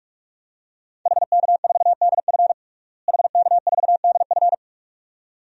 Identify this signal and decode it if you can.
Morse code, heard as a keyed tone, about 46 wpm, 700 Hz. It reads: HK4DF HK4DF
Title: DXCC-100 Call Signs - 5 Characters - Repeated 2x - 45wpm